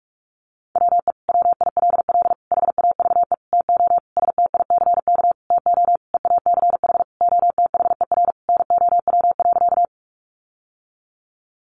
Morse code, heard as a keyed tone, about 44 wpm, 700 Hz. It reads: WE WILL HAVE TO STICK TO EACH OTHER NOW.
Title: Sentences from Top 1200 Words 45wpm